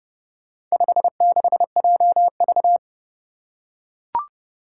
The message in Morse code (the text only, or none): 56JV